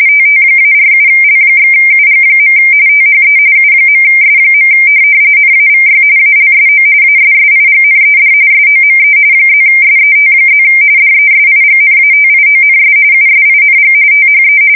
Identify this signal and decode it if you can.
RTTY, about 45.45 baud. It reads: TT-47L/UG. THE U.S. NAVY ALSO ASSIGNED SOME +SET+ DESIGNATIONS USING THE
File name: fsk170 / teletype-corp